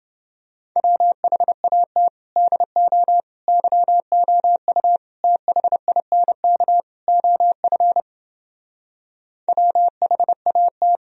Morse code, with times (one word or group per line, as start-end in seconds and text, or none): 0.8-2.1 WHAT
2.4-3.2 DO
3.5-5.0 YOU
5.2-6.8 THINK
7.1-8.0 OF
9.5-11.0 WHAT